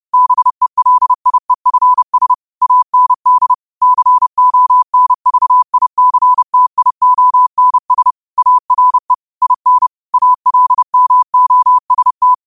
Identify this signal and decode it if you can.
Morse code, heard as a keyed tone, about 31 wpm, 1000 Hz. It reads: BELIEFS AND CONVICTIONS ARE IN ALMOST
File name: qotdfiles / 30wpm